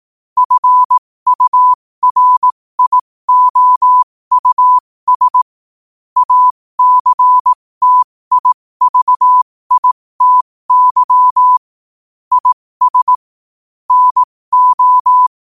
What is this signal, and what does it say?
Morse code, heard as a keyed tone, about 18 wpm, spaced as for 15 wpm, 1000 Hz. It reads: FURIOUS ACTIVITY IS NO